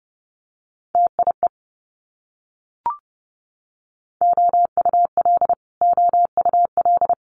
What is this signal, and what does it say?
Morse code, heard as a keyed tone, about 30 wpm, 700 Hz. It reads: TIE OUL OUL
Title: Multi-letter Phonemes Repeated 3x 30wpm